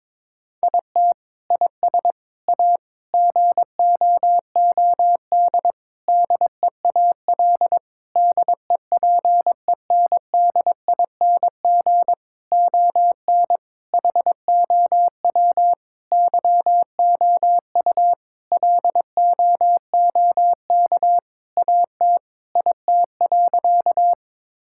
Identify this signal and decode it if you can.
Morse code, heard as a keyed tone, about 22 wpm, 700 Hz. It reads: IT IS A GOOD DEAL DEPENDING ON HOW YOU LOOK AT IT.
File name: Sentences with 7-Word Phrases - 22wpm